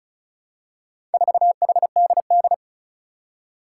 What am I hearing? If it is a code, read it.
Morse code, heard as a keyed tone, about 34 wpm, 700 Hz. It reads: 4HDD
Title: Four Challenging Letter-Number Combinations 35wpm